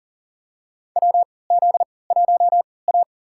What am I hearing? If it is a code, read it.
Morse code, heard as a keyed tone, about 40 wpm, spaced as for 23 wpm, 700 Hz. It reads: WZ1A
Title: USA Call Signs - 4 Characters - 3x Character Spacing - 40wpm